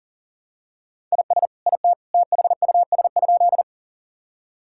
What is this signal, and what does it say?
Morse code, heard as a keyed tone, about 40 wpm, 700 Hz. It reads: IS IT THUS?
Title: Sentences from Top 1000 Words - Review Part 5 - 40wpm